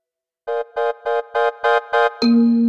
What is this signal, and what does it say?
There is no code.